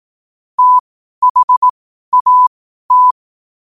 Morse code, heard as a keyed tone, about 18 wpm, spaced as for 12 wpm, 1000 Hz. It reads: THAT
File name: qotdfiles / 12wpm